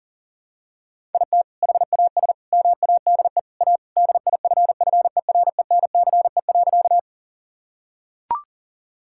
Morse code, heard as a keyed tone, about 40 wpm, 700 Hz. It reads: IT HAS MADE A DIFFERENCE.
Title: Sentences from Top 1000 Words - Review Part 5 - 40wpm